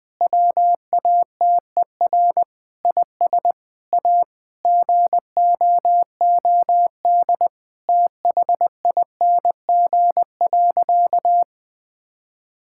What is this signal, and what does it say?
Morse code, heard as a keyed tone, about 20 wpm, 700 Hz. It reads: WATER IS A GOOD THING.
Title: Sentences with 3-Word Phrases - 20wpm